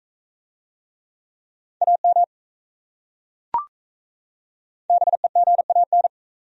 Morse code, heard as a keyed tone, about 37 wpm, 700 Hz. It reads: AM BEGAN